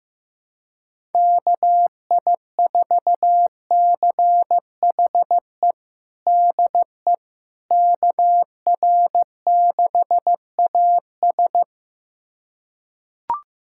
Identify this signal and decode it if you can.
Morse code, heard as a keyed tone, about 15 wpm, 700 Hz. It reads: KI4CHE DE KR6AS
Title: USA Callsign DE Callsign 15wpm